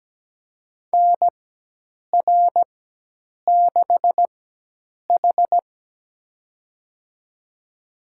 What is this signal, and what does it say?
Morse code, heard as a keyed tone, about 17 wpm, spaced as for 8 wpm, 700 Hz. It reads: NR6H
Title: USA Call Signs - 4 Characters - 4x Character Spacing - 17wpm